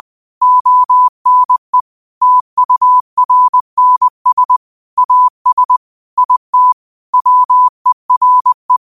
Morse code, heard as a keyed tone, about 20 wpm, 1000 Hz. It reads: ONE TURNS AS IT WERE
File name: qotdfiles / 20wpm